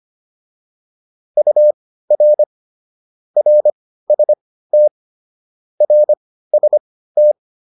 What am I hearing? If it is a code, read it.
Morse code, heard as a keyed tone, about 25 wpm, spaced as for 15 wpm, 600 Hz. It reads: UR RST RST
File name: QSO409 15